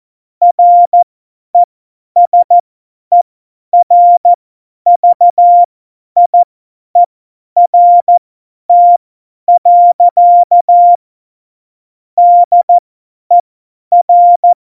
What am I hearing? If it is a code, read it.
Morse code, heard as a keyed tone, about 14 wpm, spaced as for 10 wpm, 700 Hz. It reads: RESERVIERT. DER
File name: Morsekurs / Witze / Witz 23. Juli 2019, 50 F70